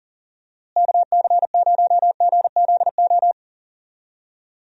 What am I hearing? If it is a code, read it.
Morse code, heard as a keyed tone, about 40 wpm, 700 Hz. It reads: KC0GZO
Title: USA Call Signs - 6 Characters - Repeated 2x - 40wpm